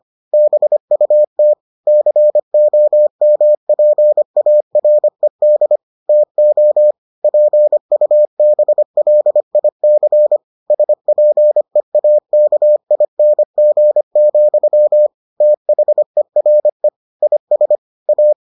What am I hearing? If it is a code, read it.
Morse code, heard as a keyed tone, about 25 wpm, 600 Hz. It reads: BUT COMPARED TO PUBLIC SPEAKING, THERE IS A